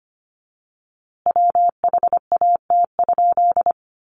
Morse code, heard as a keyed tone, about 25 wpm, 700 Hz. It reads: WHAT?
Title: A Sound Of Thunder 25wpm